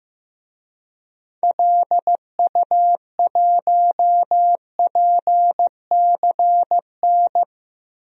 Morse code, heard as a keyed tone, about 15 wpm, 700 Hz. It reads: LU1PCN
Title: DXCC-100 Call Signs - 6 Characters - Repeated 3x - 15wpm